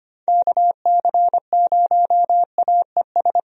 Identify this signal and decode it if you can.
Morse code, heard as a keyed tone, about 25 wpm, 700 Hz. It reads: KC0AES
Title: USA Call Signs - 6 Characters - Repeated 3x - 25wpm